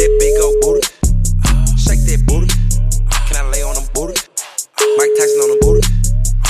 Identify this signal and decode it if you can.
Ringback tone (Europe, 425 Hz).